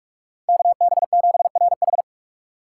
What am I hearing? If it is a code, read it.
Morse code, heard as a keyed tone, about 45 wpm, 700 Hz. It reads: KB7RH
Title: USA Call Signs - 2x Character Spacing - 45wpm